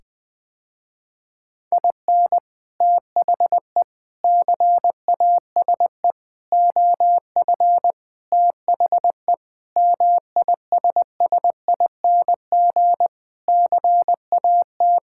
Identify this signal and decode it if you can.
Morse code, heard as a keyed tone, about 20 wpm, 690 Hz. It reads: IN THE CASE OF THE MISSING CAT
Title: Sentences with 4-Word Phrases - Spoken Beginning - 20wpm